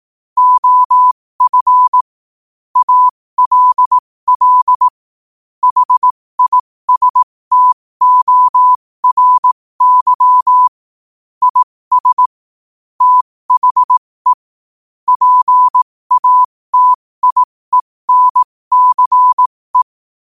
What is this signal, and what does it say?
Morse code, heard as a keyed tone, about 18 wpm, spaced as for 15 wpm, 1000 Hz. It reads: OF ALL HISTORY IS THE PATIENCE
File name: qotdfiles / 15wpm